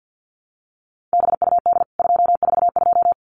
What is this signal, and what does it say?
Morse code, heard as a keyed tone, about 50 wpm, 700 Hz. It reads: BUD 242
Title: CWT Exchange 50wpm